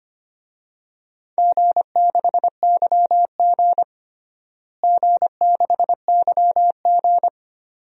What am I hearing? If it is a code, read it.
Morse code, heard as a keyed tone, about 25 wpm, 700 Hz. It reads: G6YG G6YG
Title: DXCC-100 Call Signs - 4 Characters - Repeated 2x - 25wpm